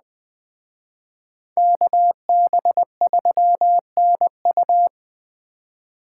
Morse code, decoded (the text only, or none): KB3NU